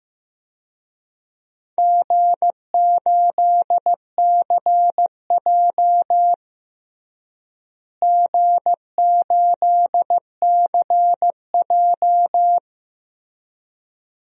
Morse code, heard as a keyed tone, about 16 wpm, 700 Hz. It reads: G8CJ G8CJ
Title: DXCC-100 Call Signs - 4 Characters - Repeated 2x - 15wpm